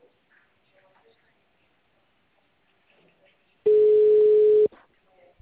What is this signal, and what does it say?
Ringback tone (Europe, 425 Hz), with speech.